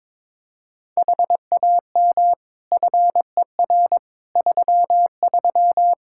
Morse code, heard as a keyed tone, about 22 wpm, 700 Hz. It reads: HAM FER 33
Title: Practice QSO - Exchange 3 - More Info - 22wpm